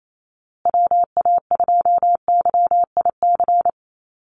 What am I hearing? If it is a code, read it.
Morse code, heard as a keyed tone, about 28 wpm, 700 Hz. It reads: WA2YIC
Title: USA Call Signs Repeated 3x 28wpm